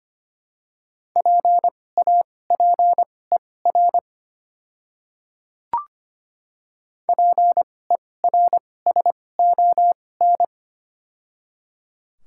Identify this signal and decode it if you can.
Morse code, heard as a keyed tone, about 25 wpm, spaced as for 18 wpm, 700 Hz. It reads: PAPER PERSON
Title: 2x Character Spacing - Top 500 Words - 25wpm